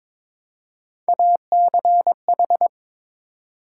Morse code, heard as a keyed tone, about 22 wpm, 700 Hz. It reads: ACH
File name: practice / 22wpm / Three Letter-Number Combinations - 22wpm